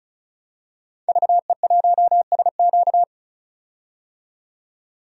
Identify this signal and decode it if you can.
Morse code, heard as a keyed tone, about 34 wpm, 700 Hz. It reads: VE1SQ
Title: DXCC-100 Call Signs - 5 Characters - 35wpm